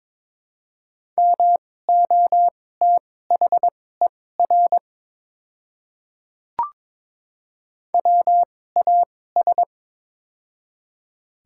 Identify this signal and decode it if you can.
Morse code, heard as a keyed tone, about 22 wpm, spaced as for 16 wpm, 700 Hz. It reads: MOTHER WAS
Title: 2x Character Spacing - Top 300 Words - 22wpm